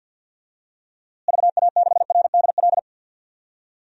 Morse code, heard as a keyed tone, about 48 wpm, 700 Hz. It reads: VA6RDL
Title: DXCC-100 Call Signs - 6 Characters - Repeated 3x - 50wpm